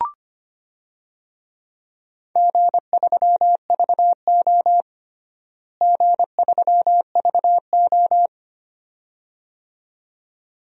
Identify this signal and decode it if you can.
Morse code, heard as a keyed tone, about 25 wpm, 700 Hz. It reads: G3VO G3VO
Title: DXCC-100 Call Signs - 4 Characters - Repeated 2x - 25wpm